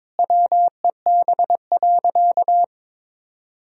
Morse code, heard as a keyed tone, about 22 wpm, 700 Hz. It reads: WEB.
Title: Sentences from Top 1000 Words - Review Part 4 - 22wpm